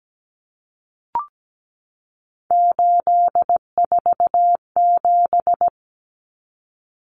Morse code, heard as a keyed tone, about 17 wpm, 700 Hz. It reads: 847
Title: Numbers - 3-Digit 17wpm